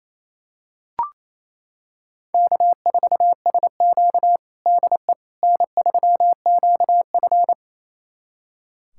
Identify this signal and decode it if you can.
Morse code, heard as a keyed tone, about 28 wpm, 700 Hz. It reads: K4SQ DE N3QF
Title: USA Callsign DE Callsign - 4 Characters - 28wpm